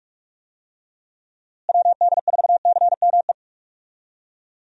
Morse code, heard as a keyed tone, about 46 wpm, 700 Hz. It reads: WD4CME
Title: USA Call Signs - 6 Characters - Repeated 3x - 45wpm